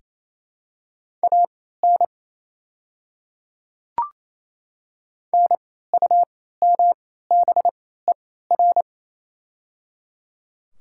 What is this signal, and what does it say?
Morse code, heard as a keyed tone, about 27 wpm, spaced as for 16 wpm, 700 Hz. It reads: AN NUMBER
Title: 3x Character Spacing - Top 100 Words - 28wpm